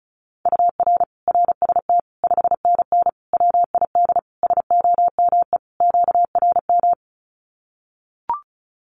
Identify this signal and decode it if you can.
Morse code, heard as a keyed tone, about 34 wpm, 700 Hz. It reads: UR RST 5NN WID SOME QRM
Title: Common QSO Phrases 35wpm